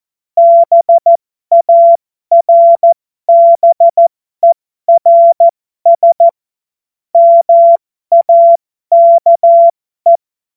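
Morse code, heard as a keyed tone, about 14 wpm, spaced as for 12 wpm, 680 Hz. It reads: BARBERS MAKE
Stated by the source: Morse code joke.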